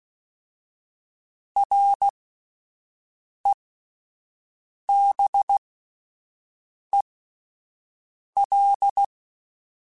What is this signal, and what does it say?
Morse code, heard as a keyed tone, about 16 wpm, spaced as for 5 wpm, 800 Hz. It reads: REBEL